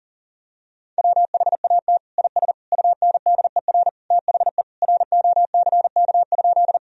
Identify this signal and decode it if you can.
Morse code, heard as a keyed tone, about 37 wpm, 700 Hz. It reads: WHAT IS UNDER THE ROCK?
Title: Sentences from Top 1000 Words - Review Part 2 - 40wpm